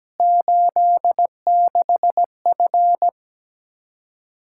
Morse code, heard as a keyed tone, about 17 wpm, 700 Hz. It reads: 86F
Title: Practice QSO - Exchange 2 - Info - 17wpm